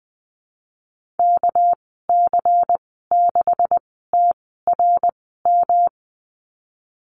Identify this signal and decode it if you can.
Morse code, heard as a keyed tone, about 20 wpm, spaced as for 15 wpm, 700 Hz. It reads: KC6TRM
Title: 238 Lesson 38g - Letter q - Call Signs 20wpm